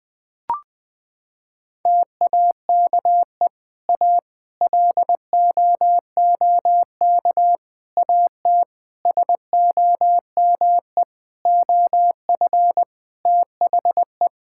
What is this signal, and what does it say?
Morse code, heard as a keyed tone, about 21 wpm, 700 Hz. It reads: TAKE A LOOK AT SOME OF THE
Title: Sentences with 7-Word Phrases - 20wpm